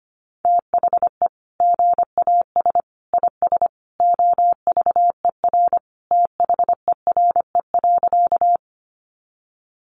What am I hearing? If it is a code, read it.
Morse code, heard as a keyed tone, about 25 wpm, 690 Hz. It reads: THE GAS IS OVER THERE.
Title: Sentences from Top 1100 Words 25wpm